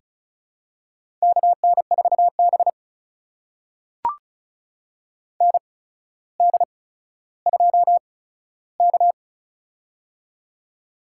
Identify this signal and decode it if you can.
Morse code, heard as a keyed tone, about 36 wpm, spaced as for 15 wpm, 700 Hz. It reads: KN4B ND2K
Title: USA Call Signs - 4 Characters - 8x Character Spacing - 35wpm